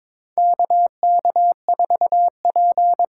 Morse code, heard as a keyed tone, about 22 wpm, 700 Hz. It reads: KK4P
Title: USA Call Signs - 6 Characters - 8x Character Spacing - 22wpm